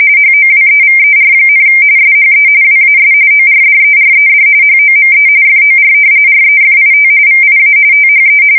RTTY, about 45.45 baud. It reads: HOWEVER, SHE WAS ORDERED HOME FOR REFIT VIA THE
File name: fsk170 / battleshipnj